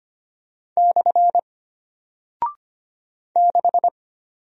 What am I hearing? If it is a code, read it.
Morse code, heard as a keyed tone, about 25 wpm, 700 Hz. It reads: / 6
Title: Single Character (Letter:Number:Punctuation:Prosign) Alt 25wpm